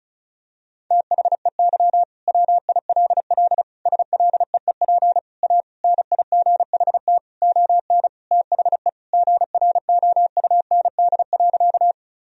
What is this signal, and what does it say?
Morse code, heard as a keyed tone, about 34 wpm, 700 Hz. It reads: THEY WILL SLEEP A NIGHT ON THE GROUND.
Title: Sentences from Top 1000 Words - Review Part 2 - 35wpm